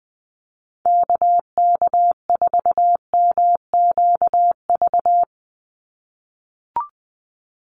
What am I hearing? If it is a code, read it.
Morse code, heard as a keyed tone, about 19 wpm, 700 Hz. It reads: KK4MQV
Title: USA Call Signs 20wpm